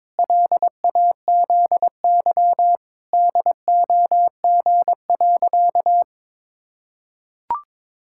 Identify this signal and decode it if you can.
Morse code, heard as a keyed tone, about 22 wpm, 700 Hz. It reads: LAZY DOG.
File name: practice / 22wpm / Sentences with 8-Word Phrases - 22wpm